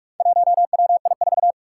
Morse code, heard as a keyed tone, about 41 wpm, 700 Hz. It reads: 1WI4
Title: Four Challenging Letter-Number Combinations 45wpm